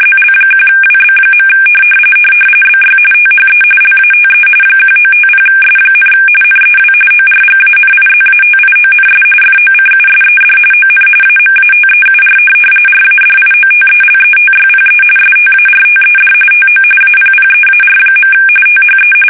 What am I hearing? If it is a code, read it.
RTTY, about 45.45 baud. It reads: ZL1WB. BY THE LATE 1950S, NEW ORGANIZATIONS FOCUSED ON AMATEUR RADIOTELETYPE STARTED TO APPEAR. THE
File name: fsk850 / teletype-RTTY